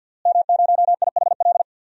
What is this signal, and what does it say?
Morse code, heard as a keyed tone, about 49 wpm, 700 Hz. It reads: M0IHL
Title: DXCC-100 Call Signs - 5 Characters - Repeated 3x - 50wpm